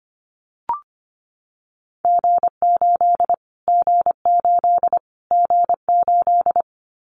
Morse code, heard as a keyed tone, about 25 wpm, 700 Hz. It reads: G8 G8 G8
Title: Callsign Prefixes for Top 25 DXCC - Repeated 3x - 25wpm